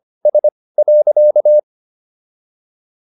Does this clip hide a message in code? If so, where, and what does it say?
Yes: Morse code from 0.3 to 1.6 s, heard as a keyed tone, about 25 wpm, spaced as for 18 wpm, 600 Hz. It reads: S.